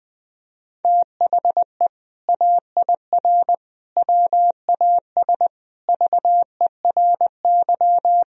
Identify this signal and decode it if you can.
Morse code, heard as a keyed tone, about 20 wpm, 700 Hz. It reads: THE AIR WAS VERY
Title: Sentences from Top 1000 Words - Review Part 3 - 20wpm